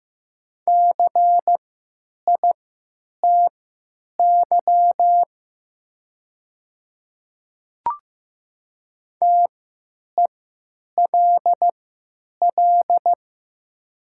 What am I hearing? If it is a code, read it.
Morse code, heard as a keyed tone, about 15 wpm, spaced as for 9 wpm, 700 Hz. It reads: CITY TELL
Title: 3x Character Spacing - Top 500 Words - 15wpm